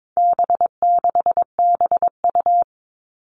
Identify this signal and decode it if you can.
Morse code, heard as a keyed tone, about 22 wpm, 700 Hz. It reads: B6BU
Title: All Cut Numbers - 4-Digit 22wpm